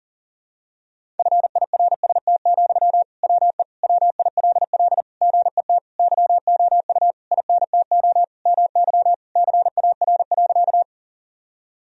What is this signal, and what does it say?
Morse code, heard as a keyed tone, about 40 wpm, 700 Hz. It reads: FIRST, WE WILL GET YOU INTO MY CAR.
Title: Sentences from Top 1000 Words - Review Part 1 - 40wpm